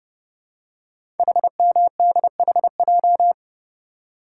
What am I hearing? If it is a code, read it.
Morse code, heard as a keyed tone, about 30 wpm, 700 Hz. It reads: HMDHJ
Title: Code Groups - 5 Letters - 30wpm